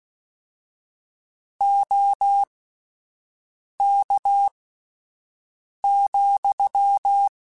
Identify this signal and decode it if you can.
Morse code, heard as a keyed tone, about 16 wpm, spaced as for 5 wpm, 800 Hz. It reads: OK,
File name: QSOs Spaced